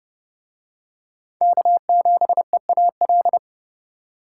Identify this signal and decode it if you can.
Morse code, heard as a keyed tone, about 29 wpm, 700 Hz. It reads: K7EAL